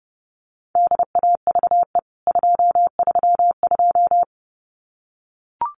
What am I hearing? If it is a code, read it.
Morse code, heard as a keyed tone, about 30 wpm, 700 Hz. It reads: DAVE 232
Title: CWT Exchange 30wpm